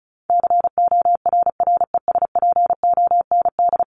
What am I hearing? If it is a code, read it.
Morse code, heard as a keyed tone, about 34 wpm, 700 Hz. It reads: CORRESPOND
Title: ING Suffix - Top 100 Words with Base Words 35wpm